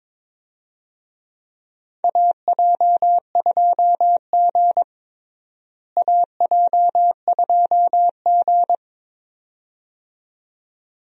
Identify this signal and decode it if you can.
Morse code, heard as a keyed tone, about 22 wpm, 700 Hz. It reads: AJ2G AJ2G
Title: USA Call Signs - 4 Characters - Repeated 2x - 22wpm